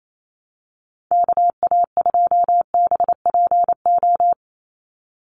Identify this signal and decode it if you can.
Morse code, heard as a keyed tone, about 28 wpm, 700 Hz. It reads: KA2BPO
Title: USA Call Signs 28wpm